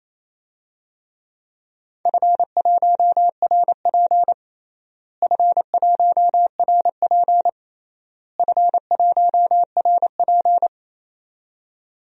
Morse code, heard as a keyed tone, about 28 wpm, 700 Hz. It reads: F1RP F1RP F1RP
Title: DXCC-100 Call Signs - 4 Characters - Repeated 3x - 28wpm